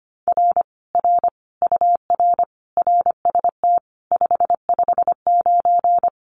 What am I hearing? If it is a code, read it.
Morse code, heard as a keyed tone, about 25 wpm, 700 Hz. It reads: R R UR RST 559
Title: Common QSO Phrases Encore 25wpm